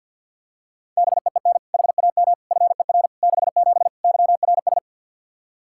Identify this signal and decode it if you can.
Morse code, heard as a keyed tone, about 53 wpm, 700 Hz. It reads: BEEN HAM FER 67 YRS
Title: Practice QSO - Exchange 3 - More Info - 50wpm